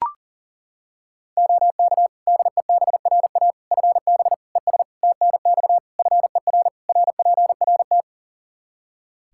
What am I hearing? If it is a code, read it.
Morse code, heard as a keyed tone, about 40 wpm, 700 Hz. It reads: OK DEBRA FB ES TNX FER RPRT
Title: Practice QSO - Exchange 2 - Info - 40wpm